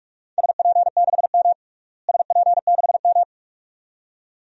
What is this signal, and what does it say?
Morse code, heard as a keyed tone, about 46 wpm, 700 Hz. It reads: SP6M SP6M
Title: DXCC-100 Call Signs - 4 Characters - Repeated 2x - 45wpm